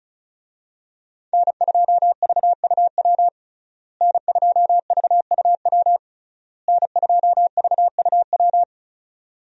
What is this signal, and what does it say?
Morse code, heard as a keyed tone, about 36 wpm, 700 Hz. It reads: N2VUW N2VUW N2VUW
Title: USA Call Signs - 5 Characters - Repeated 3x - 35wpm